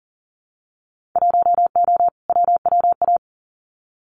Morse code, heard as a keyed tone, about 40 wpm, 700 Hz. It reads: 1O WWA
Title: Winter Field Day Exchanges 40wpm